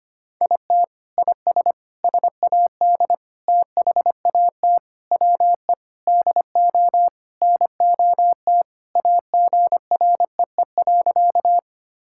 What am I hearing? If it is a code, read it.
Morse code, heard as a keyed tone, about 25 wpm, 700 Hz. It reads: IT IS SAD THAT WE DO NOT AGREE.
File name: practice / 25wpm / Sentences with 4-Word Phrases - 25wpm